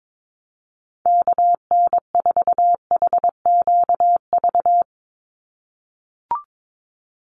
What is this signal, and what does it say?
Morse code, heard as a keyed tone, about 22 wpm, 700 Hz. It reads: KN4HQV
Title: USA Call Signs 22wpm